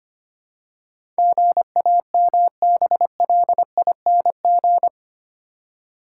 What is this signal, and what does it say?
Morse code, heard as a keyed tone, about 25 wpm, 700 Hz. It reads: GAMBLING
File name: Eight Letter Words - 25wpm